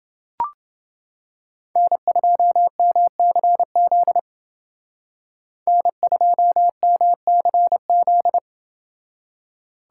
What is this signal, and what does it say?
Morse code, heard as a keyed tone, about 29 wpm, 700 Hz. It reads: N2MCZ N2MCZ